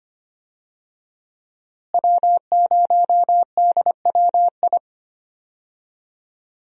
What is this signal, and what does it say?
Morse code, heard as a keyed tone, about 25 wpm, 700 Hz. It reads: W0DWI
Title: USA Call Signs - 5 Characters - Repeated 2x - 25wpm